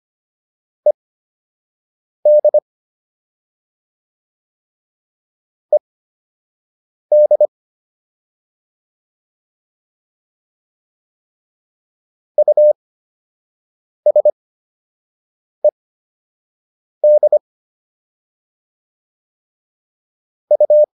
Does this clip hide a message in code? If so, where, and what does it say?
Yes: Morse code from 0.9 to 20.8 s, heard as a keyed tone, about 25 wpm, spaced as for 6 wpm, 600 Hz. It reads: ED ED USED U